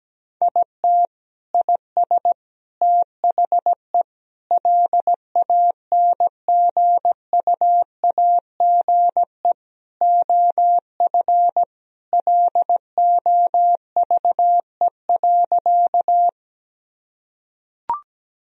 Morse code, heard as a keyed tone, about 17 wpm, 700 Hz. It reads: IT IS THE LANGUAGE OF LOVE.
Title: Sentences from Top 1000 Words - Review Part 3 - 17wpm